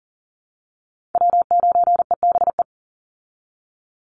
Morse code, heard as a keyed tone, about 40 wpm, 700 Hz. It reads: W9EBE
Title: Call Signs Repeated 3x - 40wpm